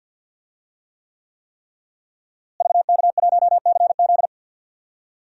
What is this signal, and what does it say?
Morse code, heard as a keyed tone, about 52 wpm, 700 Hz. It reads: VK1CZ
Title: DXCC-100 Call Signs - 5 Characters - Repeated 3x - 50wpm